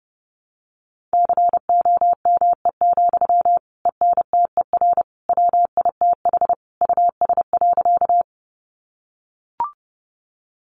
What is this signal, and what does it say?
Morse code, heard as a keyed tone, about 30 wpm, 700 Hz. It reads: COME, ENTER WITH US.
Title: Sentences from Top 1300 Words 30wpm